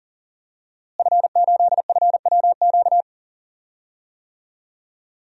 Morse code, heard as a keyed tone, about 40 wpm, 700 Hz. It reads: F8FWQ
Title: DXCC-100 Call Signs - 5 Characters - Repeated 3x - 40wpm